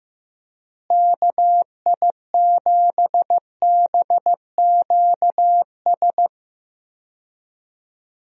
Morse code, heard as a keyed tone, about 15 wpm, 700 Hz. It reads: KI7BQS
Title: USA Call Signs - 6 Characters - Repeated 3x - 15wpm